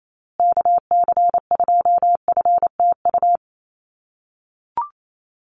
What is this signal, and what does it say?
Morse code, heard as a keyed tone, about 28 wpm, 700 Hz. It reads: KC2FTU
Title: USA Call Signs Repeated 2x 28wpm